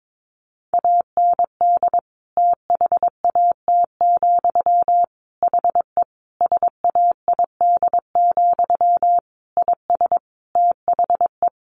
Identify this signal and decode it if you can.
Morse code, heard as a keyed tone, about 22 wpm, 700 Hz. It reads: AND THAT, HE SAID, IS THE
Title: A Sound Of Thunder 22wpm